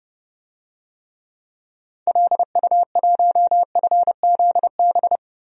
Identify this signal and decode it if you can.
Morse code, heard as a keyed tone, about 32 wpm, 700 Hz. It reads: LU1FZB
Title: DXCC-100 Call Signs - 6 Characters - 30wpm